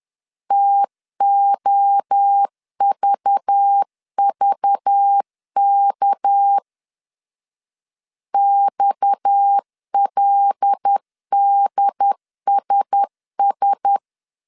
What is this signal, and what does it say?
Morse code, heard as a keyed tone, about 10 wpm, 800 Hz. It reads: TOVVK XLDSS
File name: CW / DET / 1 9 1500C S C 10wpm 4557